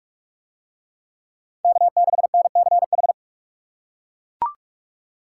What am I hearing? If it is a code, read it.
Morse code, heard as a keyed tone, about 40 wpm, 700 Hz. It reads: K6NCH